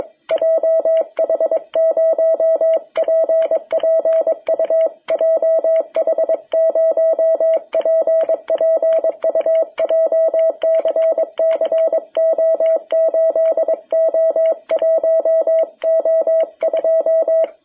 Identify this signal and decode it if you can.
Morse code, heard as a keyed tone, about 22 wpm, 660 Hz. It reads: JH0PPU JH0PPU JCC O8O1O2
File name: jh8cbh / 2021cw